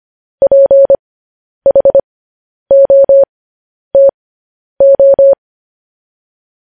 Morse code, heard as a keyed tone, about 25 wpm, spaced as for 10 wpm, 550 Hz. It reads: PHOTO
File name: podcast / News headlines-1x-25x10